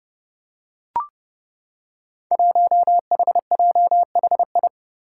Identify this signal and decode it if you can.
Morse code, heard as a keyed tone, about 30 wpm, 700 Hz. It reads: 1HJHI
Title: Five Challenging Letter-Number Combinations 30wpm